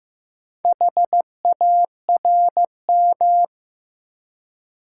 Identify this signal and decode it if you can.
Morse code, heard as a keyed tone, about 15 wpm, 700 Hz. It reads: HARM
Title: Four Letter Words - 15wpm